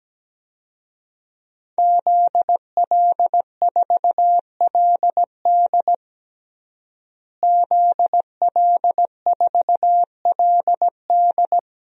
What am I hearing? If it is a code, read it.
Morse code, heard as a keyed tone, about 17 wpm, 700 Hz. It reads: ZL4LD ZL4LD
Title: DXCC-100 Call Signs - 5 Characters - Repeated 3x - 17wpm